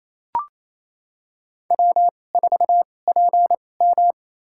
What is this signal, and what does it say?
Morse code, heard as a keyed tone, about 28 wpm, spaced as for 20 wpm, 700 Hz. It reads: W4PM
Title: USA Call Signs - 4 Characters - 2x Character Spacing - 28wpm